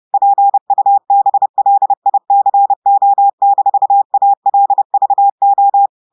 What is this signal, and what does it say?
Morse code, heard as a keyed tone, about 30 wpm, 800 Hz. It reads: PUBLICO-ALVO
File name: audios / portugais / portugais texte en clair 2 vitesse 30 mots minutes